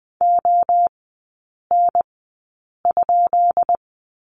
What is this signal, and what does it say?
Morse code, heard as a keyed tone, about 20 wpm, spaced as for 8 wpm, 700 Hz. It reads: ON?
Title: 218 Lesson 35e - Number 0 - Three Characters 20wpm